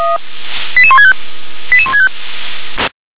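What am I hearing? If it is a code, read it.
Five-tone selective call (ZVEI), heard as quick runs of five tones, tones of about 70 ms. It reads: 88165 88165
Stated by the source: ZVEI five-tone call received by SDR.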